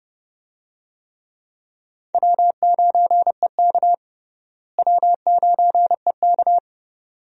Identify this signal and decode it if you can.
Morse code, heard as a keyed tone, about 30 wpm, 700 Hz. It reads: W9EK W9EK
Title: USA Call Signs - 4 Characters - Repeated 2x - 30wpm